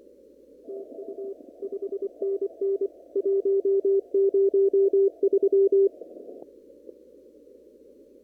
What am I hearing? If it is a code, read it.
Morse code, heard as a keyed tone, about 24 wpm, 390 Hz. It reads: TU 5NN 103
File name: CW / kt5j wpx cw-op-k5pi